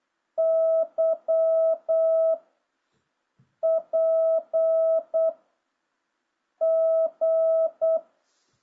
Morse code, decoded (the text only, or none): Y P G